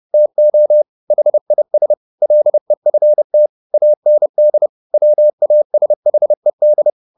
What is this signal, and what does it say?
Morse code, heard as a keyed tone, about 30 wpm, 600 Hz. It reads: TO HIS LEFT AND WASHED